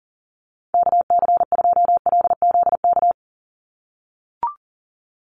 Morse code, heard as a keyed tone, about 40 wpm, 690 Hz. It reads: KC2LZK
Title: USA Call Signs 40wpm